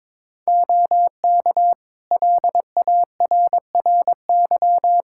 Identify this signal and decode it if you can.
Morse code, heard as a keyed tone, about 22 wpm, 700 Hz. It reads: OK LARRY
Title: Practice QSO - Exchange 3 - More Info - 22wpm